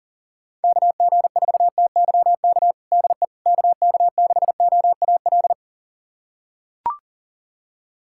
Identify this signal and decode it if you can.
Morse code, heard as a keyed tone, about 40 wpm, 700 Hz. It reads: KG4TYK DE KK6OAL
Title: USA Callsign DE Callsign - 6 Characters - 40wpm